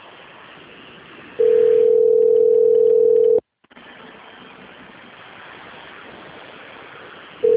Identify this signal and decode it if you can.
Ringback tone (North America), 440 + 480 Hz.
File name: out- - - - - .25074